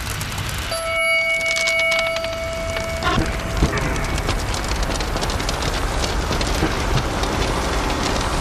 Dial-up modem handshake.